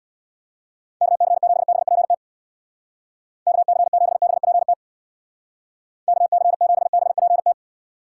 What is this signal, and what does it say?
Morse code, heard as a keyed tone, about 62 wpm, 700 Hz. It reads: KC7BPT KC7BPT KC7BPT
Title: Speed Racing - USA Call Signs 50wpm